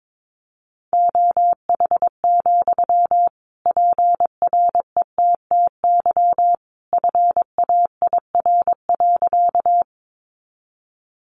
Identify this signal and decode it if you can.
Morse code, heard as a keyed tone, about 22 wpm, 700 Hz. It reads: OH, PRETTY FAIR.
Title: Sentences from Top 1300 Words 22wpm